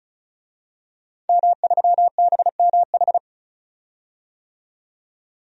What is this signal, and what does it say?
Morse code, heard as a keyed tone, about 36 wpm, 700 Hz. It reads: M3BMH
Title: DXCC-100 Call Signs - 5 Characters - 35wpm